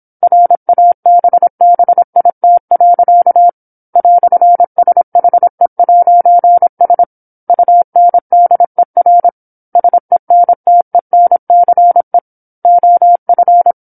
Morse code, heard as a keyed tone, about 27 wpm, 700 Hz. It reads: RABBIT. "SHE'S UNDER SENTENCE OF
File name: 08-Chapter-08 26 WPM